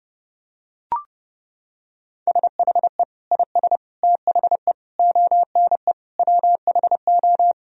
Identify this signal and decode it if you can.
Morse code, heard as a keyed tone, about 30 wpm, 700 Hz. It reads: SHE IS THE ONE WHO
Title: Sentences from Top 1000 Words - Review Part 2 - 30wpm